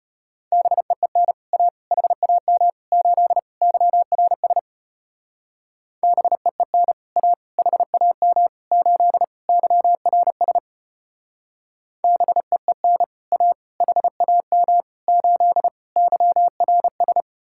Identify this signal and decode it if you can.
Morse code, heard as a keyed tone, about 32 wpm, 700 Hz. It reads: BEEN A HAM 8 YRS BEEN A HAM 8 YRS BEEN A HAM 8 YRS